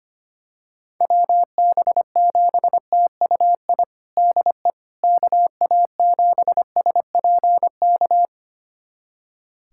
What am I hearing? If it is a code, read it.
Morse code, heard as a keyed tone, about 25 wpm, 700 Hz. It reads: WB7TUI DE KA7SPK
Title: USA Callsign DE Callsign 25wpm